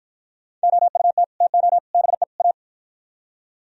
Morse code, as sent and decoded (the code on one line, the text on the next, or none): --- ..- - / - --- / -... / .-
OUT TO BE A